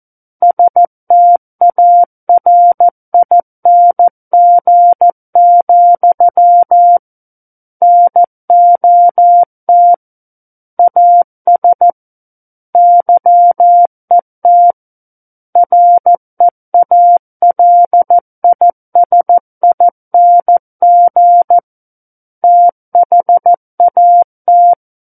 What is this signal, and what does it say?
Morse code, heard as a keyed tone, about 14 wpm, 700 Hz. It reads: STARING, NOT AS YET REALISING THAT
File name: Book One / 05-Chapter 5 - 14 WPM